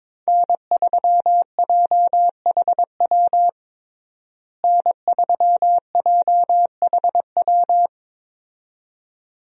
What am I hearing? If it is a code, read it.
Morse code, heard as a keyed tone, about 21 wpm, 700 Hz. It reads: N3JHW N3JHW
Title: USA Call Signs - 5 Characters - Repeated 2x - 22wpm